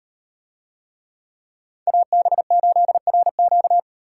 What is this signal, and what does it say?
Morse code, heard as a keyed tone, about 36 wpm, 700 Hz. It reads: AB8RQ